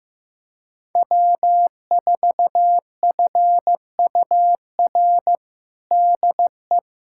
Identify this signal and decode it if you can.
Morse code, heard as a keyed tone, about 15 wpm, 700 Hz. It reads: W4FUR DE
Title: USA Callsign DE Callsign - 5 Characters - 15wpm